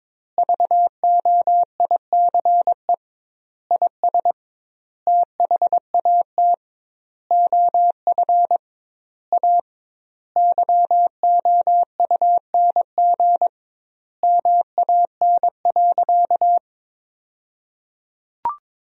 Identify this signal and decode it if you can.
Morse code, heard as a keyed tone, about 22 wpm, 700 Hz. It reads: VOICE IS THAT OF A YOUNG MAN.
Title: 2x Word-Spacing - Sentences from Top 500 Words - 22wpm